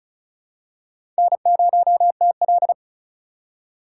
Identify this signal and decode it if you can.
Morse code, heard as a keyed tone, about 36 wpm, 700 Hz. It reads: N0TL
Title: USA Call Signs - 4 Characters - 3x Character Spacing - 35wpm